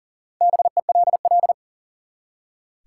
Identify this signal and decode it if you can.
Morse code, heard as a keyed tone, about 40 wpm, 700 Hz. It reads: BELL
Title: Four Letter Words - 40wpm